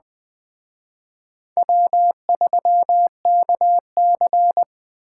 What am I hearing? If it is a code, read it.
Morse code, heard as a keyed tone, about 21 wpm, 700 Hz. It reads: W3KC
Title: Speed Racing - USA Call Signs 17wpm